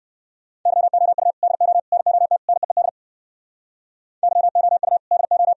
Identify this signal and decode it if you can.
Morse code, heard as a keyed tone, about 68 wpm, 700 Hz. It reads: YOU DO NOT NEED YOU DO